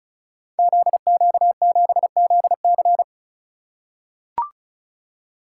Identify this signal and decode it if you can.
Morse code, heard as a keyed tone, about 34 wpm, 700 Hz. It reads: ZQ7ZC
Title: Code Groups - 5 Letter:Numbers - 35wpm